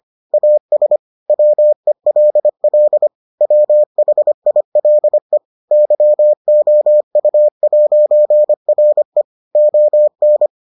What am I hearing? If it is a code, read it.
Morse code, heard as a keyed tone, about 25 wpm, 600 Hz. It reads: AS WELL WHILE YOU'RE ON